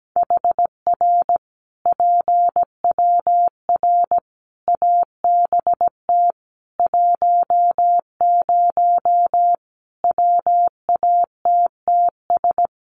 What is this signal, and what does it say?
Morse code, heard as a keyed tone, about 17 wpm, 700 Hz. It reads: HR PWR ABT 10 WATTS
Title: Common QSO Phrases 17wpm